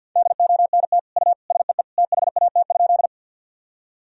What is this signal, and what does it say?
Morse code, heard as a keyed tone, about 47 wpm, 700 Hz. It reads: DONT U SEE THAT?
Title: Sentences from Top 300 Words and Abbreviations 50wpm